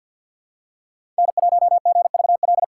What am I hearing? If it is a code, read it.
Morse code, heard as a keyed tone, about 52 wpm, 700 Hz. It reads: N1GVL
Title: USA Call Signs - 5 Characters - 4x Character Spacing - 50wpm